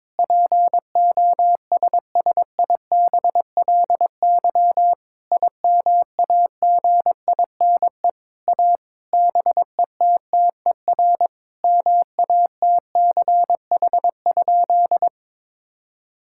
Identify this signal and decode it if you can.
Morse code, heard as a keyed tone, about 22 wpm, 700 Hz. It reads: POSSIBLY IMAGINE A BETTER MATCH?
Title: Sentences from Top 1600 Words 22wpm